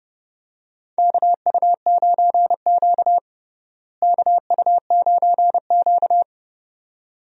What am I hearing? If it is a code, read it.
Morse code, heard as a keyed tone, about 30 wpm, 700 Hz. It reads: KU9Q KU9Q
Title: USA Call Signs - 4 Characters - Repeated 2x - 30wpm